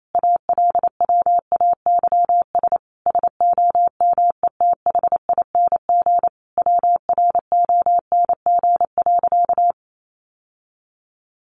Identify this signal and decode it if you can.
Morse code, heard as a keyed tone, about 28 wpm, 690 Hz. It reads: ALWAYS SOMETHING WRONG.
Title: Sentences from Top 900 Words 28wpm